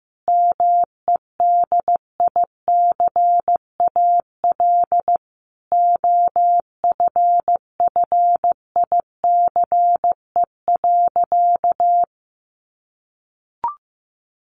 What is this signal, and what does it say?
Morse code, heard as a keyed tone, about 15 wpm, 700 Hz. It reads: MEDICAL OFFICE.
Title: Additional Easy Phrases 15wpm